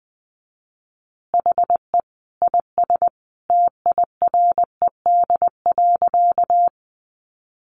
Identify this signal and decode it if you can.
Morse code, heard as a keyed tone, about 20 wpm, 700 Hz. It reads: HE IS TIRED.
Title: Additional Easy Phrases 20wpm